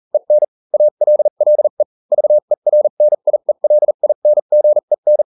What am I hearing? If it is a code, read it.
Morse code, heard as a keyed tone, about 40 wpm, spaced as for 35 wpm, 600 Hz. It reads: EN ALLE VERNIELINGEN